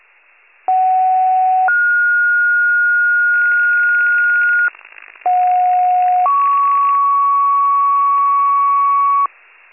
Two-tone page, heard as a one-second tone, then a three-second tone, about 750 then 1430 Hz, about 730 then 1120 Hz, with noise. It is heard from 0.7 to 9.3 s.
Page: 83 RESCUE DUTY